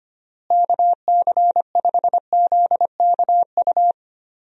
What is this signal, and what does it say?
Morse code, heard as a keyed tone, about 25 wpm, 700 Hz. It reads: KC5ZKU
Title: USA Call Signs - 6 Characters - Repeated 3x - 25wpm